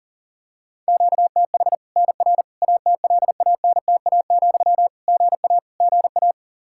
Morse code, heard as a keyed tone, about 40 wpm, 700 Hz. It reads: QTH NR ATLANTA, GA GA
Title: K1USN SST QSO Practice 40wpm